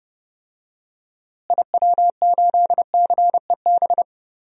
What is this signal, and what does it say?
Morse code, heard as a keyed tone, about 30 wpm, 700 Hz. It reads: IW8CEB
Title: DXCC-100 Call Signs - 6 Characters - Repeated 2x - 30wpm